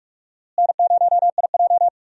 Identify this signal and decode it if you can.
Morse code, heard as a keyed tone, about 46 wpm, 700 Hz. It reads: N0IJ
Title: USA Call Signs - 4 Characters - Repeated 3x - 45wpm